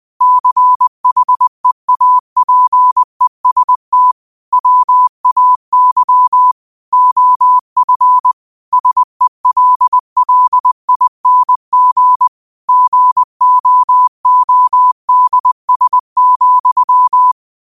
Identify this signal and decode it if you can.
Morse code, heard as a keyed tone, about 20 wpm, 1000 Hz. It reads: CHEAPEST WAY OF SELLING GOODS,
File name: qotdfiles / 20wpm